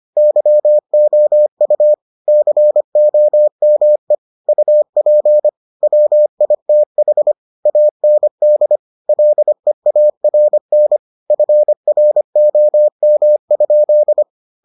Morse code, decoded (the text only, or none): YOU COME UP WITH AND LEARN FROM?